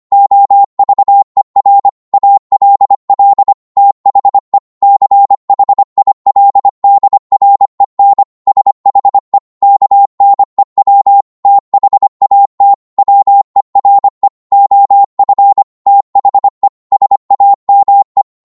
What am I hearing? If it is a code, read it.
Morse code, heard as a keyed tone, about 25 wpm, 800 Hz. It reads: OVER ALL THE CHILDREN SHE KNEW THAT WERE OF THE SAME